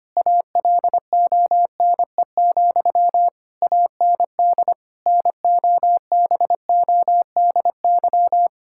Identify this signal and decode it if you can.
Morse code, heard as a keyed tone, about 25 wpm, 700 Hz. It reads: ALONE, AND NOBODY
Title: Sentences from Top 1600 Words 25wpm